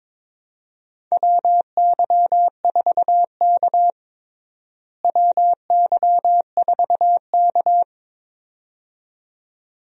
Morse code, heard as a keyed tone, about 22 wpm, 700 Hz. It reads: WY4K WY4K
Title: USA Call Signs - 4 Characters - Repeated 2x - 22wpm